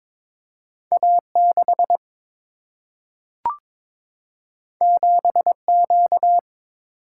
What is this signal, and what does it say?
Morse code, heard as a keyed tone, about 22 wpm, 700 Hz. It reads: A6 7Q
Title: Single-Letter Single-Number 22wpm